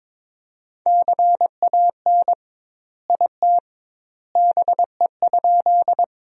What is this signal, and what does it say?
Morse code, heard as a keyed tone, about 22 wpm, 700 Hz. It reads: CAN IT BE?
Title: 2x Word-Spacing - Sentences from Top 100 Words - 22wpm